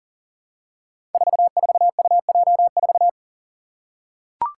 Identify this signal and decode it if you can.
Morse code, heard as a keyed tone, about 38 wpm, 700 Hz. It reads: 44UJ4
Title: Five Challenging Letter-Number Combinations 40wpm